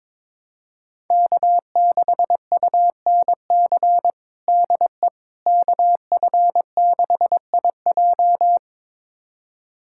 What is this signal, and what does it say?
Morse code, heard as a keyed tone, about 22 wpm, 700 Hz. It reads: K6UNC DE KF6IJ
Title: USA Callsign DE Callsign - 5 Characters - 22wpm